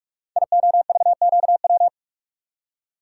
Morse code, heard as a keyed tone, about 44 wpm, 700 Hz. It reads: IOVQW
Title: Code Groups - 5 Letter:Numbers - 45wpm